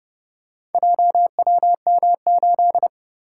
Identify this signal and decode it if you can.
Morse code, heard as a keyed tone, about 31 wpm, 700 Hz. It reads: JWM8